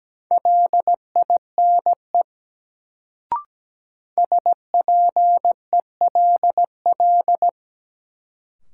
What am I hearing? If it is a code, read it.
Morse code, heard as a keyed tone, about 17 wpm, 700 Hz. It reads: LINE SPELL
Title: Sentences from Top 1000 Words - Review Part 1 - New Word Drill - 17wpm